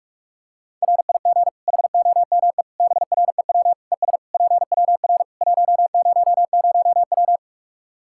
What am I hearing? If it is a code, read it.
Morse code, heard as a keyed tone, about 41 wpm, 700 Hz. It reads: RIG HOME BREW ES PWR 100W
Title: Practice QSO - Exchange 2 - Info - 45wpm